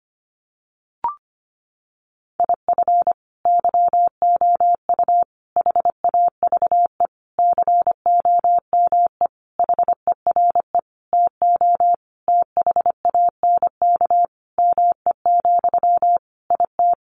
Morse code, heard as a keyed tone, about 25 wpm, 700 Hz. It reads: IF YOU HAVE COME HERE TO THANK ME, IT
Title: Sentences from Top 800 Words 25wpm